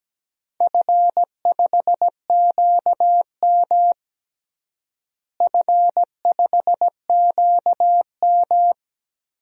Morse code, heard as a keyed tone, about 17 wpm, 700 Hz. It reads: F5QM F5QM
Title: DXCC-100 Call Signs - 4 Characters - Repeated 2x - 17wpm